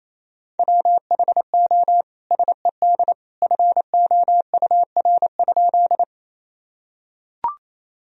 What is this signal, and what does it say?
Morse code, heard as a keyed tone, about 28 wpm, 700 Hz. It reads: WHO SED FOUR?
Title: Sentences from Top 300 Words and Abbreviations 28wpm